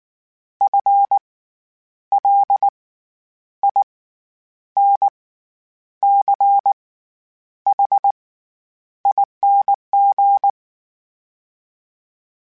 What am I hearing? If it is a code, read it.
Morse code, heard as a keyed tone, about 19 wpm, spaced as for 9 wpm, 800 Hz. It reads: FLINCHING
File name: Special INGs / Special INGs-mp3s / SlowToFaster ing @18